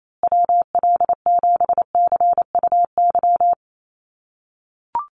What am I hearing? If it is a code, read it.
Morse code, heard as a keyed tone, about 28 wpm, 700 Hz. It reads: WL7CUY
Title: USA Call Signs Repeated 2x 28wpm